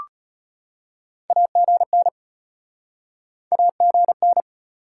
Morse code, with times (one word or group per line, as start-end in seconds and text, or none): 1.3-2.1 AGN
3.5-4.4 AGN